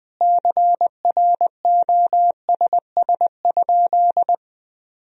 Morse code, heard as a keyed tone, about 21 wpm, 700 Hz. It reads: CROSS?
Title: Sentences from Top 1000 Words - Review Part 1 - 20wpm